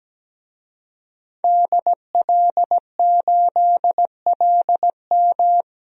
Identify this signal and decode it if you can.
Morse code, heard as a keyed tone, about 17 wpm, 700 Hz. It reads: DL8LM
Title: DXCC-100 Call Signs - 5 Characters - 17wpm